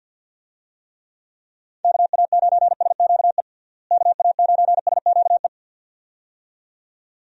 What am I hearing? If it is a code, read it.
Morse code, heard as a keyed tone, about 46 wpm, 700 Hz. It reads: KA9SQE KA9SQE
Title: USA Call Signs - 6 Characters - Repeated 2x - 50wpm